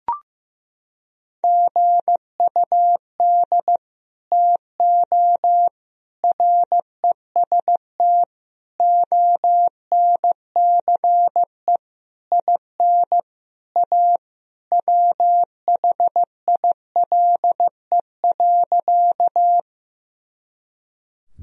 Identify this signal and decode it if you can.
Morse code, heard as a keyed tone, about 15 wpm, 700 Hz. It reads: GUD TO REST ONCE IN A WHILE.
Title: Sentences from Top 500 Words and Abbreviations 15wpm